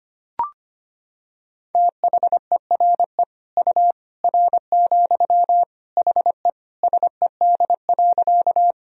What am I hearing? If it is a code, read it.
Morse code, heard as a keyed tone, about 25 wpm, 700 Hz. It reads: THERE U R, HE SED.
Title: Sentences from Top 100 Words and Abbreviations 25wpm